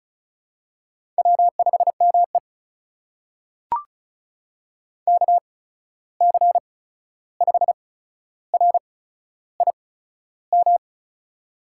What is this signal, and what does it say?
Morse code, heard as a keyed tone, about 34 wpm, spaced as for 13 wpm, 700 Hz. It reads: W5ME KC5RIM
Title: USA Call Signs - 8x Character Spacing - 35wpm